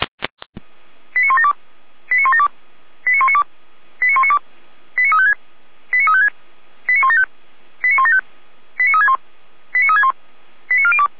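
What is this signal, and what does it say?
Five-tone selective call (ZVEI), heard as quick runs of five tones, tones of about 70 ms. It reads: 89172 89172 89192 89192 89356 89356 89175 89175 89361 89361 89402
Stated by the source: ZVEI five-tone call received on an SDR, recorded by the Feuerwehr Orlamünde fire brigade.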